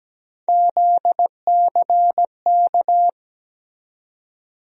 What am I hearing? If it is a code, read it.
Morse code, heard as a keyed tone, about 17 wpm, 700 Hz. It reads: ZCK
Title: USA Call Sign Postfixes - 3 Letters 17wpm